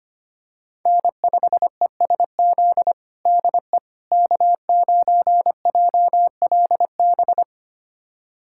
Morse code, heard as a keyed tone, about 25 wpm, 700 Hz. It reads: N5ESZ DE K9JLB
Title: USA Callsign DE Callsign - 5 Characters - 25wpm